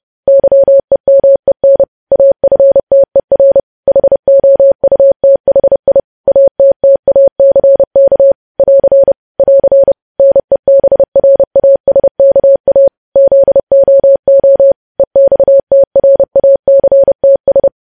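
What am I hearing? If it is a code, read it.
Morse code, heard as a keyed tone, about 30 wpm, 550 Hz. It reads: YEMEN AFTER HOUTHI ATTACK <AR> <AR> NEBRASKA ZOO EXTRACTS
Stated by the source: Morse news headlines sent at 30 words per minute.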